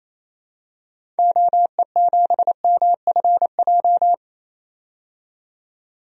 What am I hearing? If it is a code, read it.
Morse code, heard as a keyed tone, about 28 wpm, 700 Hz. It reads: OE7MFJ
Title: DXCC-100 Call Signs - 6 Characters - 28wpm